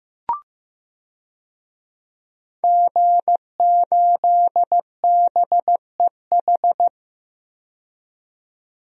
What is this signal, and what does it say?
Morse code, heard as a keyed tone, about 15 wpm, 700 Hz. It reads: G8BEH
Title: DXCC-100 Call Signs - 5 Characters - 15wpm